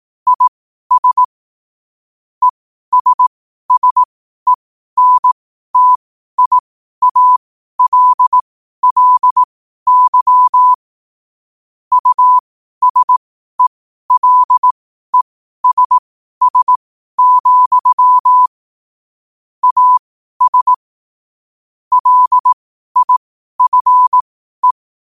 Morse code, heard as a keyed tone, about 18 wpm, spaced as for 12 wpm, 1000 Hz. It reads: IS ESSENTIALLY USELESS, AS LIFE
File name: qotdfiles / 12wpm